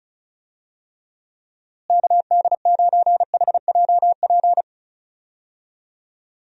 Morse code, heard as a keyed tone, about 34 wpm, 700 Hz. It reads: KD9HJP
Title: USA Call Signs - 6 Characters - Repeated 2x - 35wpm